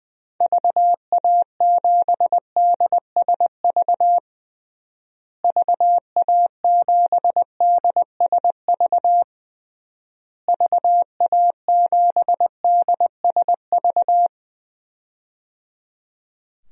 Morse code, heard as a keyed tone, about 20 wpm, 700 Hz. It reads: VA7DSV VA7DSV VA7DSV
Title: DXCC-100 Call Signs - 6 Characters - Repeated 3x - 20wpm